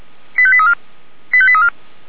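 Five-tone selective call (ZVEI), heard as quick runs of five tones, tones of about 70 ms. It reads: 85834 85834
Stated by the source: ZVEI five-tone call 89484 received by SDR, from feuerwehr-orlamuende fire brigade server.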